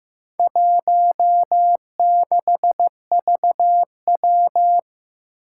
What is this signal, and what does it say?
Morse code, heard as a keyed tone, about 15 wpm, 700 Hz. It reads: 16VW
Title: Four Challenging Letter-Number Combinations 15wpm